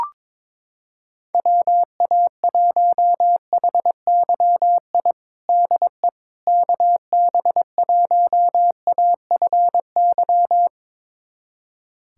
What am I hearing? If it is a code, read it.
Morse code, heard as a keyed tone, about 22 wpm, 700 Hz. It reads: WA1HYI DE KB1AFY
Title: USA Callsign DE Callsign - 6 Characters - 22wpm